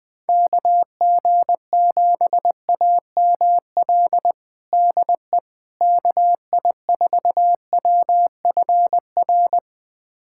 Morse code, heard as a keyed tone, about 21 wpm, 700 Hz. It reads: KG7AML DE KI4WFR
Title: USA Callsign DE Callsign 20wpm